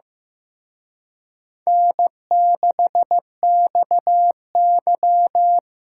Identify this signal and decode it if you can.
Morse code, heard as a keyed tone, about 18 wpm, 700 Hz. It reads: N6XY
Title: Speed Racing - USA Call Signs 15wpm